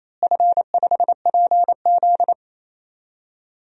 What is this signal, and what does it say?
Morse code, heard as a keyed tone, about 28 wpm, 700 Hz. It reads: F5PZ
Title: DXCC-100 Call Signs - 4 Characters - Repeated 3x - 28wpm